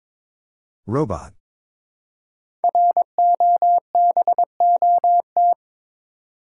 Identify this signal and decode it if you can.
Morse code, heard as a keyed tone, about 22 wpm, 700 Hz. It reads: ROBOT
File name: 5 Letter Words - First 3 Letters Given - 22wpm